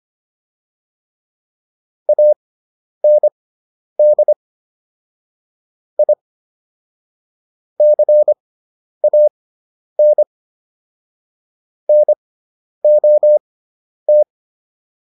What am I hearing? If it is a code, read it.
Morse code, heard as a keyed tone, about 25 wpm, spaced as for 10 wpm, 600 Hz. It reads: AND I CAN NOT